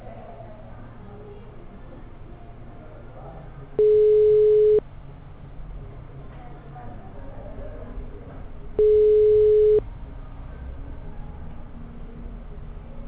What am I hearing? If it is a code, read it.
Ringback tone (Europe, 425 Hz), with speech.
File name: out- -2004- - - .27814